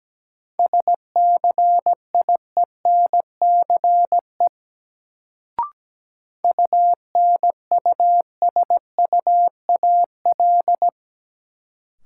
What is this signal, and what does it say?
Morse code, heard as a keyed tone, about 17 wpm, 700 Hz. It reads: SCIENCE UNUSUAL
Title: Sentences from Top 1400 Words - New Word Drill Alt - 17wpm